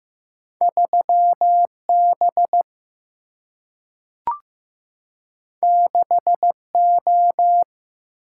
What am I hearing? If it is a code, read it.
Morse code, heard as a keyed tone, about 15 wpm, 700 Hz. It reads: 3B 6O
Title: Single-Letter Single-Number 15wpm